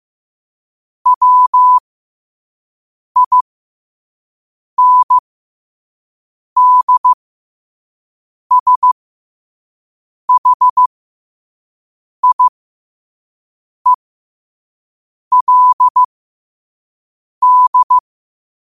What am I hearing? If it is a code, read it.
Morse code, heard as a keyed tone, about 15 wpm, spaced as for 5 wpm, 1000 Hz. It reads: WINDSHIELD
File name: qotdfiles / 5wpm